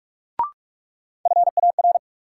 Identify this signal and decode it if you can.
Morse code, heard as a keyed tone, about 46 wpm, 700 Hz. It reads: FAR
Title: Sentences from Top 300 Words - New Word Drill - 45wpm